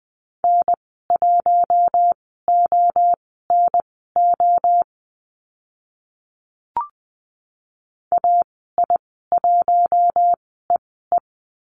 Morse code, heard as a keyed tone, about 20 wpm, spaced as for 14 wpm, 700 Hz. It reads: N1ONO AI1EE
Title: 042 Lesson 8g - Number 1 - Call Signs 20wpm